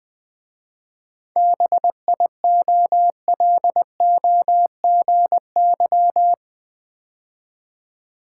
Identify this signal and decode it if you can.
Morse code, heard as a keyed tone, about 20 wpm, 700 Hz. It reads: BIOLOGY
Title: 7 Letter Words - First 3 Letters Given - 20wpm